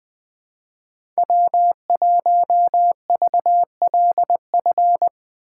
Morse code, heard as a keyed tone, about 21 wpm, 700 Hz. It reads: W1VLF